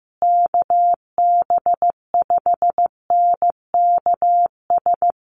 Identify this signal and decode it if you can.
Morse code, heard as a keyed tone, about 14 wpm, 700 Hz. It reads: KB5NKS
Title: USA Call Signs Repeated 3x 15wpm